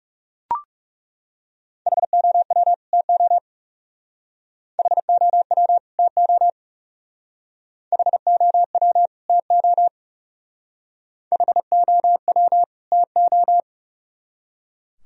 Morse code, heard as a keyed tone, about 38 wpm, 700 Hz. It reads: HOW TO HOW TO HOW TO HOW TO